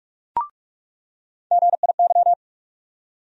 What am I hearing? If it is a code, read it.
Morse code, heard as a keyed tone, about 44 wpm, 700 Hz. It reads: GIY
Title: USA Call Sign Postfixes - 3 Letters 45wpm